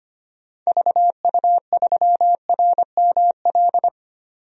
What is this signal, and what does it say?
Morse code, heard as a keyed tone, about 25 wpm, 700 Hz. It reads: VU3RML
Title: DXCC-100 Call Signs - 6 Characters - 25wpm